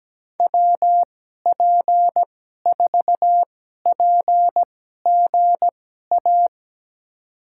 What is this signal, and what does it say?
Morse code, heard as a keyed tone, about 17 wpm, spaced as for 12 wpm, 700 Hz. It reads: WP4PGA
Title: USA Call Signs - 6 Characters - 2x Character Spacing - 17wpm